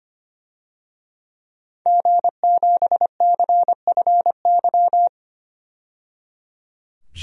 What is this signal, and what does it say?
Morse code, heard as a keyed tone, about 25 wpm, 700 Hz. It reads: G7CFY
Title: DXCC-100 Call Signs - 5 Characters - 25wpm